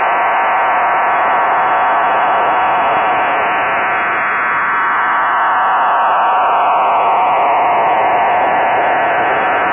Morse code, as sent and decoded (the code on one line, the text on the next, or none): none